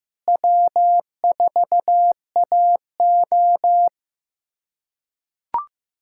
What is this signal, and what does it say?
Morse code, heard as a keyed tone, about 16 wpm, 700 Hz. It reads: W4AO
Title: USA Callsign DE Callsign - 4 Characters - 15wpm